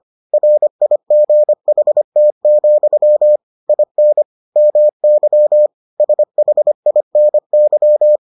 Morse code, heard as a keyed tone, about 25 wpm, 600 Hz. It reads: RIGHT, IN MY SHINY